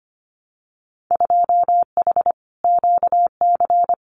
Morse code, heard as a keyed tone, about 25 wpm, 700 Hz. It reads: 2H QC
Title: Winter Field Day Exchanges 25wpm